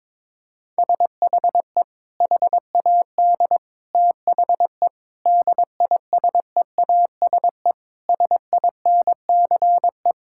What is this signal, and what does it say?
Morse code, heard as a keyed tone, about 22 wpm, 700 Hz. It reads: SHE HAD THE DISEASE SINCE
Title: Sentences from Top 1000 Words - Review Part 6 - 22wpm